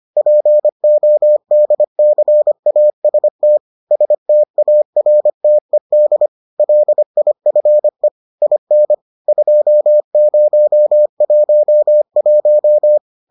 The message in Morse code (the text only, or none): PODCAST STARTED LIFE IN 2011